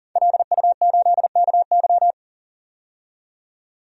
Morse code, heard as a keyed tone, about 40 wpm, 700 Hz. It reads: LU8KY
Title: DXCC-100 Call Signs - 5 Characters - 40wpm